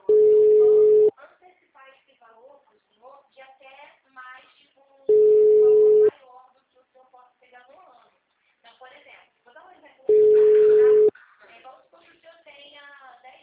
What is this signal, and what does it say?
Ringback tone (Europe, 425 Hz), with speech.
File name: out- -2033- - - .2184